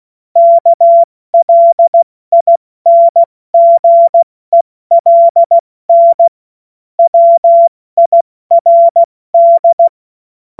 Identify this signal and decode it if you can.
Morse code, heard as a keyed tone, about 16 wpm, 680 Hz. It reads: KLINGELN WIRD
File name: Morsekurs / Witze / Witz 14. Februar 2023 70 F80